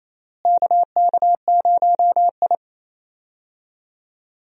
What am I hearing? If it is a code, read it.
Morse code, heard as a keyed tone, about 28 wpm, 700 Hz. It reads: KK0I
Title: USA Call Signs - 4 Characters - Repeated 3x - 28wpm